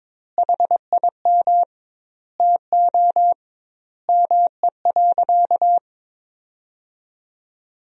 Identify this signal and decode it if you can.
Morse code, heard as a keyed tone, about 22 wpm, 700 Hz. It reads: HIM TO ME.
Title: 2x Word-Spacing - Sentences from Top 200 Words - 22wpm